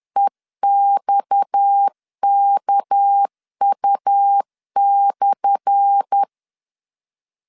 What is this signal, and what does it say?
Morse code, heard as a keyed tone, about 10 wpm, 800 Hz. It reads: EXKU/
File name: LEZIONI / 10 SIGNS / 10A SLASH 10wpm 2808